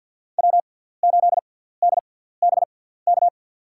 Morse code, heard as a keyed tone, about 50 wpm, spaced as for 17 wpm, 700 Hz. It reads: W8DBK
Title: USA Call Signs - 6x Character Spacing - 50wpm